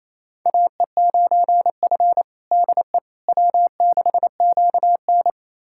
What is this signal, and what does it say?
Morse code, heard as a keyed tone, about 28 wpm, 700 Hz. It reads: AE9F DE W6QN
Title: USA Callsign DE Callsign - 4 Characters - 28wpm